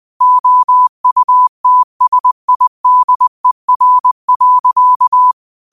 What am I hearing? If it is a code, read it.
Morse code, heard as a keyed tone, about 20 wpm, 1000 Hz. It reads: OUTSIDER.
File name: qotdfiles / 20wpm